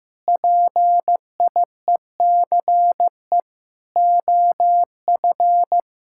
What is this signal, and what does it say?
Morse code, heard as a keyed tone, about 16 wpm, 700 Hz. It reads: PIECE OF
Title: Sentences from Top 1600 Words 15wpm